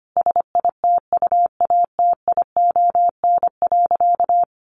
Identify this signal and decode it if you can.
Morse code, heard as a keyed tone, about 25 wpm, 700 Hz. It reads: SITUATION.
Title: Sentences from Top 1300 Words 25wpm